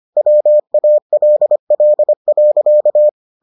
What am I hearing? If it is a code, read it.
Morse code, heard as a keyed tone, about 25 wpm, 600 Hz. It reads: WALL.